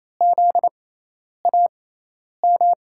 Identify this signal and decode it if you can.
Morse code, heard as a keyed tone, about 28 wpm, spaced as for 10 wpm, 700 Hz. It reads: ZAM
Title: USA Call Signs - 5 Characters - 6x Character Spacing - 28wpm